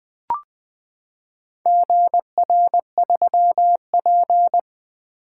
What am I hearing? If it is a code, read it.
Morse code, heard as a keyed tone, about 20 wpm, 700 Hz. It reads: GR3P
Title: Four Letter-Number Combinations - 20wpm